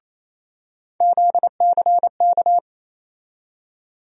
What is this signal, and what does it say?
Morse code, heard as a keyed tone, about 28 wpm, 700 Hz. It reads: ZCK
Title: USA Call Sign Postfixes - 3 Letters 28wpm